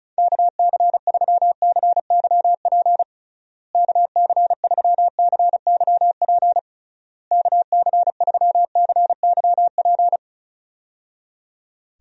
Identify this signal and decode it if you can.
Morse code, heard as a keyed tone, about 36 wpm, 700 Hz. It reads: KC3CYP KC3CYP KC3CYP
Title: USA Call Signs - 6 Characters - Repeated 3x - 35wpm